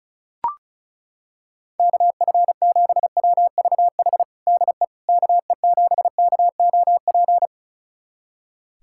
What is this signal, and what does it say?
Morse code, heard as a keyed tone, about 36 wpm, 700 Hz. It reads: KF7WVH DE KE7KOP
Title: USA Callsign DE Callsign - 6 Characters - 35wpm